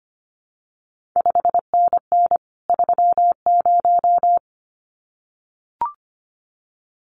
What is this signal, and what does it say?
Morse code, heard as a keyed tone, about 25 wpm, 700 Hz. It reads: 5NN 30
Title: CQ WW DX Contest Exchange Encore - 25wpm